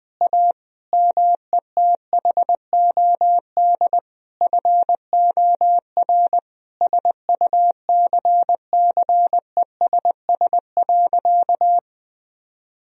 Morse code, heard as a keyed tone, about 20 wpm, 700 Hz. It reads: A METHOD FOR SUCCESS.
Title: Sentences from Top 1000 Words - Review Part 3 - 20wpm